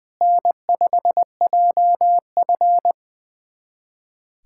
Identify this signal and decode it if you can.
Morse code, heard as a keyed tone, about 21 wpm, 700 Hz. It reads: N5JF
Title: USA Callsign DE Callsign - 4 Characters - 20wpm